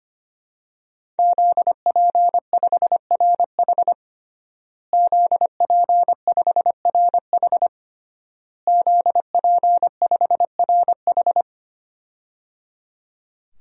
Morse code, heard as a keyed tone, about 25 wpm, 700 Hz. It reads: ZP5RH ZP5RH ZP5RH
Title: DXCC-100 Call Signs - 5 Characters - Repeated 3x - 25wpm